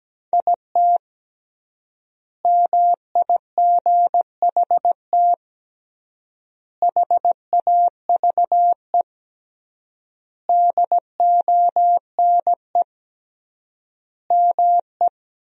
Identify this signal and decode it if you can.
Morse code, heard as a keyed tone, about 17 wpm, 700 Hz. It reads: IT MIGHT HAVE DONE ME
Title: 3x Word-Spacing - Sentences from Top 500 Words - 17wpm